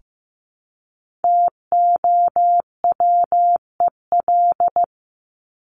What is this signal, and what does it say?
Morse code, heard as a keyed tone, about 15 wpm, 700 Hz. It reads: TOWEL
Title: Top 1501-1750 Words 15wpm